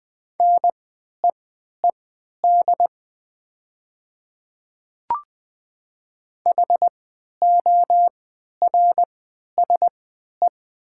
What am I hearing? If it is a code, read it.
Morse code, heard as a keyed tone, about 20 wpm, spaced as for 11 wpm, 700 Hz. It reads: NEED HORSE
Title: 3x Character Spacing - Top 500 Words - 20wpm